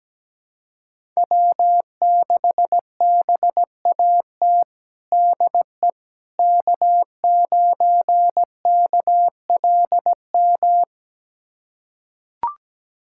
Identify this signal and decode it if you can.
Morse code, heard as a keyed tone, about 17 wpm, 700 Hz. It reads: W6BAT DE K9KLM
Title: USA Callsign DE Callsign - 5 Characters - 17wpm